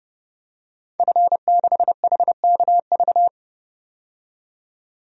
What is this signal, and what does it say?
Morse code, heard as a keyed tone, about 30 wpm, 690 Hz. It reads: F6HKV
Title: DXCC-100 Call Signs - 5 Characters - 30wpm